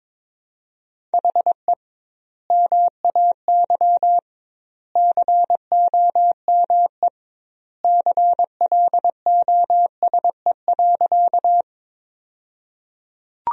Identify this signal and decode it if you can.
Morse code, heard as a keyed tone, about 22 wpm, 700 Hz. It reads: HE MAY COME CLOSE.
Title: 2x Word-Spacing - Sentences from Top 300 Words - 22wpm